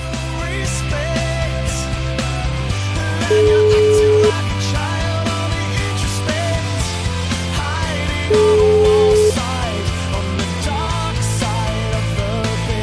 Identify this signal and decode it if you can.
Ringback tone (Europe, 425 Hz), with noise.